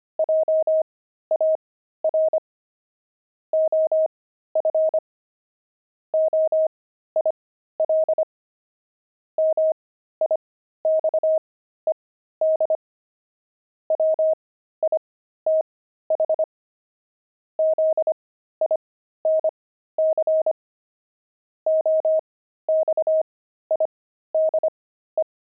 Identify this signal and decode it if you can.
Morse code, heard as a keyed tone, about 25 wpm, spaced as for 13 wpm, 630 Hz. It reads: JAR OF OIL MIXED WITH ZINC OXIDE
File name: pangram-fs-13-cs-25-oxide